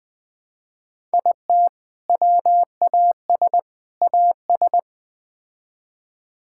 Morse code, heard as a keyed tone, about 20 wpm, 700 Hz. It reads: IT WAS AS
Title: Build A Phrase - 4-Word Phrases - 20wpm